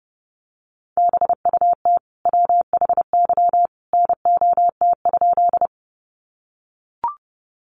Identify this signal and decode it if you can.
Morse code, heard as a keyed tone, about 30 wpm, 700 Hz. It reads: BUT WHY NOT?
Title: Sentences from Top 600 Words 30wpm